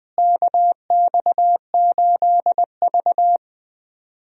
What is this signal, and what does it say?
Morse code, heard as a keyed tone, about 20 wpm, 700 Hz. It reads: KX8V
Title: USA Call Signs - 4 Characters - Repeated 2x - 20wpm